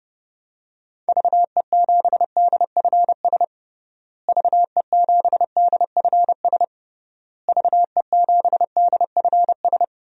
Morse code, heard as a keyed tone, about 30 wpm, 700 Hz. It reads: VE7DFS VE7DFS VE7DFS
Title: DXCC-100 Call Signs - 6 Characters - Repeated 3x - 30wpm